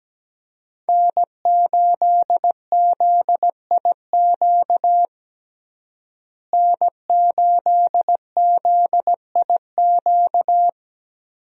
Morse code, heard as a keyed tone, about 17 wpm, 700 Hz. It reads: N8ZIQ N8ZIQ
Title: USA Call Signs - 5 Characters - Repeated 2x - 17wpm